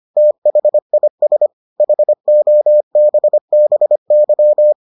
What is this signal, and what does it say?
Morse code, heard as a keyed tone, about 25 wpm, 600 Hz. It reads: THIS HOBBY